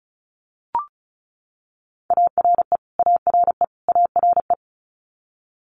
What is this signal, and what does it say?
Morse code, heard as a keyed tone, about 34 wpm, 700 Hz. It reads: ARE ARE ARE
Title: Top 500 Words Repeated 35wpm